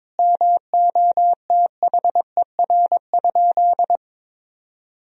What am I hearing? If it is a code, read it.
Morse code, heard as a keyed tone, about 22 wpm, 700 Hz. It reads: MOTHER?
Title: Sentences from Top 1000 Words - Review Part 4 - 22wpm